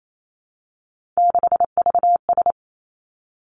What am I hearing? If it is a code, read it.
Morse code, heard as a keyed tone, about 28 wpm, 700 Hz. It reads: 6VS
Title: Three Challenging Letter-Number Combinations 28wpm